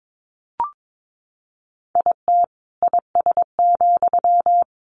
Morse code, heard as a keyed tone, about 22 wpm, 700 Hz. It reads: IT IS,
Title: Sentences from Top 1400 Words 22wpm